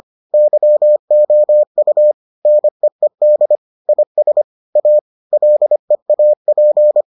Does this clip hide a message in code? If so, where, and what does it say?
Yes: Morse code from 0.3 to 7.0 s, heard as a keyed tone, about 25 wpm, 600 Hz. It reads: YOU NEED IS A LEAP